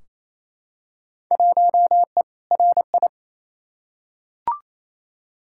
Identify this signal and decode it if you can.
Morse code, heard as a keyed tone, about 28 wpm, 700 Hz. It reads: 1E RI
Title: Field Day Exchanges Encore - 28wpm